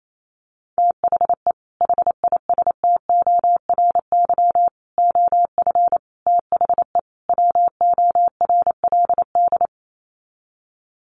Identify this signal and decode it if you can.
Morse code, heard as a keyed tone, about 28 wpm, 700 Hz. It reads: THE HISTORY OF THE WORLD
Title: Sets of 5 Words Encore2 28wpm